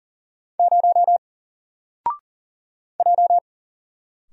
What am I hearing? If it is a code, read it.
Morse code, heard as a keyed tone, about 40 wpm, 700 Hz. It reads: 0 J
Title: Single Letter-Number 40wpm